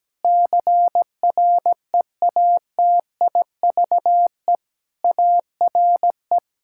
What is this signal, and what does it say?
Morse code, heard as a keyed tone, about 17 wpm, 700 Hz. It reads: CREATIVE ARE
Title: Sentences from Top 1000 Words - Review Part 5 - 17wpm